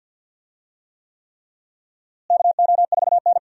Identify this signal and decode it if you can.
Morse code, heard as a keyed tone, about 47 wpm, 700 Hz. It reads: KO4N
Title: USA Call Signs - 4 Characters - Repeated 3x - 50wpm